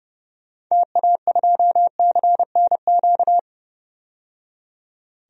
Morse code, heard as a keyed tone, about 30 wpm, 700 Hz. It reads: TA2CNQ
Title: DXCC-100 Call Signs - 6 Characters - 30wpm